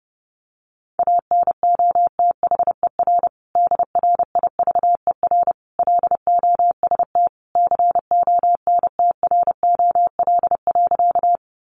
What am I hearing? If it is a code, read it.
Morse code, heard as a keyed tone, about 30 wpm, 700 Hz. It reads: ANOTHER DRIVER LOST CONTROL.
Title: Sentences from Top 1400 Words 30wpm